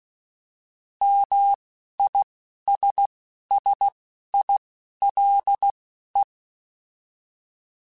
Morse code, heard as a keyed tone, about 16 wpm, spaced as for 11 wpm, 800 Hz. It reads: MISSILE